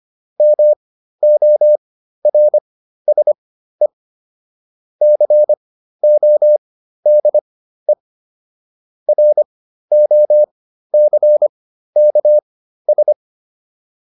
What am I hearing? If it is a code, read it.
Morse code, heard as a keyed tone, about 25 wpm, spaced as for 13 wpm, 600 Hz. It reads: MORSE CODE ROCKS